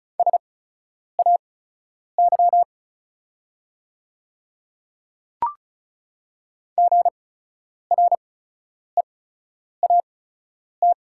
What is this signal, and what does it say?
Morse code, heard as a keyed tone, about 36 wpm, spaced as for 10 wpm, 690 Hz. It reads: SAY GREAT
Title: 8x Character Spacing - Top 300 Words - 35wpm